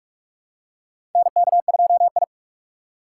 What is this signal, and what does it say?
Morse code, heard as a keyed tone, about 46 wpm, 700 Hz. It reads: NK2I
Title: USA Call Signs - 2x Character Spacing - 45wpm